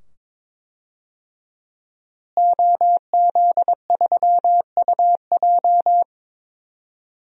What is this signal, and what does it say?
Morse code, heard as a keyed tone, about 22 wpm, 700 Hz. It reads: OZ3UJ
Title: DXCC-100 Call Signs - 5 Characters - Repeated 2x - 22wpm